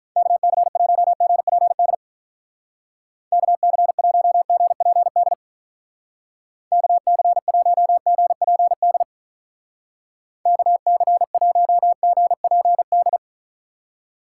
Morse code, heard as a keyed tone, about 44 wpm, 700 Hz. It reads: KC1GPD KC1GPD KC1GPD KC1GPD